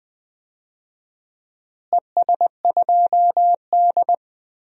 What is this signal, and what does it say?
Morse code, heard as a keyed tone, about 20 wpm, 700 Hz. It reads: ES2D
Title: DXCC-100 Call Signs - 4 Characters - Repeated 3x - 20wpm